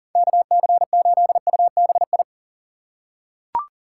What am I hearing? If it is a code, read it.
Morse code, heard as a keyed tone, about 37 wpm, 700 Hz. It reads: KC8UBI
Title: USA Call Signs - 6 Characters - 6x Character Spacing - 40wpm